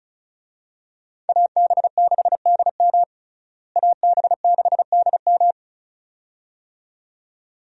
Morse code, heard as a keyed tone, about 36 wpm, 700 Hz. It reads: AB6DM AB6DM
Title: USA Call Signs - 5 Characters - Repeated 2x - 35wpm